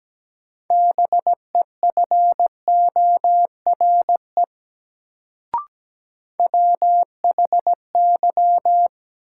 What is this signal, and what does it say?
Morse code, heard as a keyed tone, about 17 wpm, 700 Hz. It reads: BEFORE WHY
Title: Sentences from Top 200 Words - New Word Drill - 17wpm